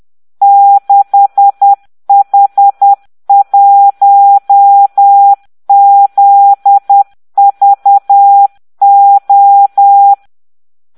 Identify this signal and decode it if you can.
Morse code, heard as a keyed tone, about 10 wpm, 800 Hz. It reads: 6H1ZVO